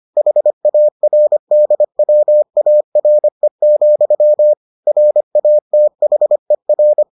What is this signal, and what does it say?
Morse code, heard as a keyed tone, about 25 wpm, 600 Hz. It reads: HARDWARE, RATHER